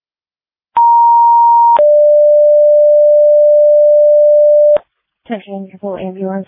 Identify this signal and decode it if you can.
Two-tone page, heard as a one-second tone, then a three-second tone, about 940 then 590 Hz, with speech. It is heard from 0.8 to 4.8 s.